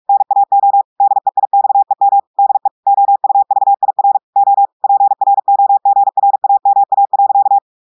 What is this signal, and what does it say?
Morse code, heard as a keyed tone, about 46 wpm, 800 Hz. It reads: NAO DEIXEM DE OUVIR O PROGRAMA.
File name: audios / portugais / portugais texte en clair 2 vitesse 45 mots minutes